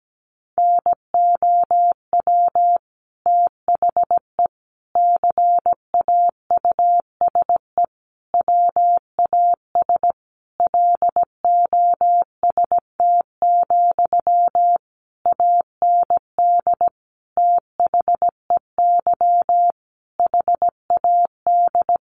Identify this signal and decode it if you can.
Morse code, heard as a keyed tone, about 17 wpm, 700 Hz. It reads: NOW THE CAUSE WAS LOST, AND THEY HAD
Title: Sentences from Top 1500 Words 17wpm